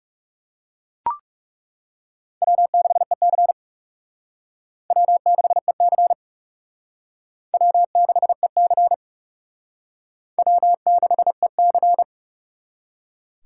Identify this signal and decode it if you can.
Morse code, heard as a keyed tone, about 38 wpm, 700 Hz. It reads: W6EC W6EC W6EC W6EC